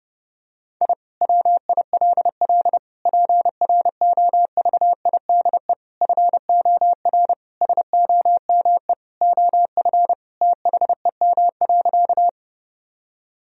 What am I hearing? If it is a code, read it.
Morse code, heard as a keyed tone, about 32 wpm, 700 Hz. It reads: I WILL PROVIDE FOR SOME OF THEM.
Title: Sentences from Top 1000 Words - Review Part 3 - 30wpm